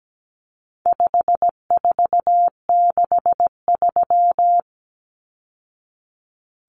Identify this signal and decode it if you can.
Morse code, heard as a keyed tone, about 17 wpm, 700 Hz. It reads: 5463
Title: Numbers - 4-Digit 17wpm